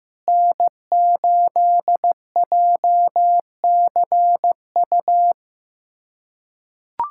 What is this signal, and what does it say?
Morse code, heard as a keyed tone, about 15 wpm, 700 Hz. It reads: N8JCU
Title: USA Callsign DE Callsign - 5 Characters - 15wpm